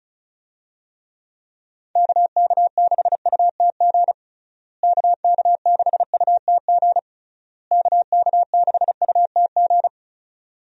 Morse code, heard as a keyed tone, about 34 wpm, 700 Hz. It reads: KK6UTG KK6UTG KK6UTG
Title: USA Call Signs - 6 Characters - Repeated 3x - 35wpm